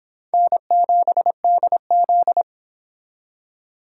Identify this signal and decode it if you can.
Morse code, heard as a keyed tone, about 25 wpm, 700 Hz. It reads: N7DZ
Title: Speed Racing - USA 1x2 Call Signs 22wpm